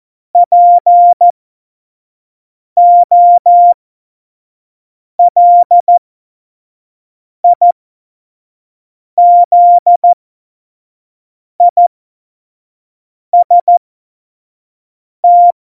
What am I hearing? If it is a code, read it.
Morse code, heard as a keyed tone, about 14 wpm, spaced as for 5 wpm, 700 Hz. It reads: POLIZIST
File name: Morsekurs / Witze / Witz 26. Februar 2019, 25 F70